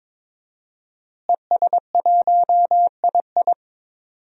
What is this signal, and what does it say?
Morse code, heard as a keyed tone, about 23 wpm, 700 Hz. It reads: ES1II
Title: DXCC-100 Call Signs - 5 Characters - Repeated 3x - 22wpm